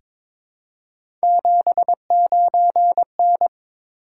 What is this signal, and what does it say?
Morse code, heard as a keyed tone, about 22 wpm, 700 Hz. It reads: 79N
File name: practice / 22wpm / Three Letter-Number Combinations - 22wpm